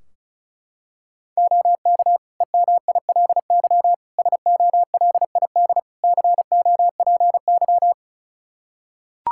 Morse code, heard as a keyed tone, about 36 wpm, 700 Hz. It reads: OK EMILY SOLID COPY
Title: Practice QSO - Exchange 3 - More Info - 35wpm